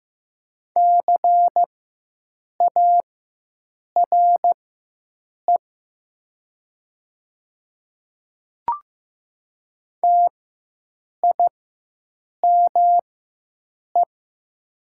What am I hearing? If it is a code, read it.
Morse code, heard as a keyed tone, about 15 wpm, spaced as for 7 wpm, 700 Hz. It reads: CARE TIME
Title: 4x Character Spacing - Top 500 Words - 15wpm